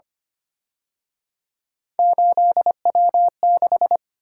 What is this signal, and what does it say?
Morse code, heard as a keyed tone, about 29 wpm, 700 Hz. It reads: 8W6
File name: Speed-Racing - Three Letter-Number Combinations - 25wpm